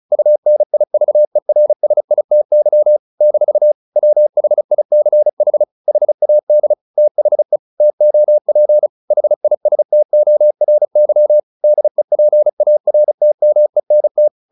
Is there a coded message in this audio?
Yes: Morse code from 0.1 to 14.3 s, heard as a keyed tone, about 36 wpm, 600 Hz. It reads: UNIVERSITY - WHICH HAD THE TOP HISTORY DEPARTMENT